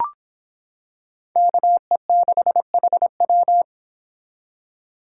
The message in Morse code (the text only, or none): KE6HW